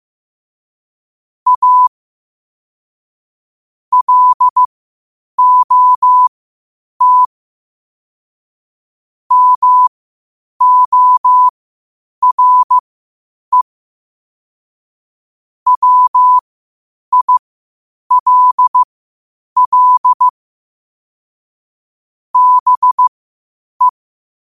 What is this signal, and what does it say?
Morse code, heard as a keyed tone, about 15 wpm, spaced as for 8 wpm, 1000 Hz. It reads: A LOT MORE WILL BE
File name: qotdfiles / 8wpm